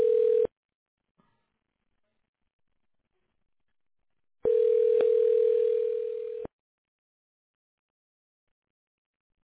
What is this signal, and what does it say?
Ringback tone (North America), 440 + 480 Hz.